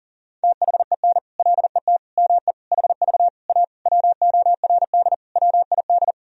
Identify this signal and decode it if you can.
Morse code, heard as a keyed tone, about 37 wpm, 700 Hz. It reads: THEN LET ME HV A WORD WID
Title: Sentences from Top 300 Words and Abbreviations 40wpm